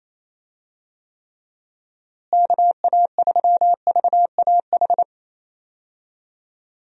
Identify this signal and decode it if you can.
Morse code, heard as a keyed tone, about 27 wpm, 700 Hz. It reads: KA3VAH
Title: USA Call Signs - 6 Characters - Repeated 2x - 28wpm